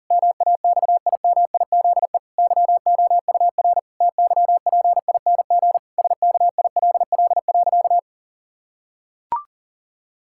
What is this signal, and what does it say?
Morse code, heard as a keyed tone, about 40 wpm, 700 Hz. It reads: MAXIMIZE YOUR TYPING SKILL.
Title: Pangram Challenge v2 - 40wpm